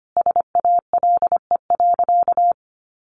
Morse code, heard as a keyed tone, about 24 wpm, 700 Hz. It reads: SALE.
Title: Sentences from Top 1400 Words 25wpm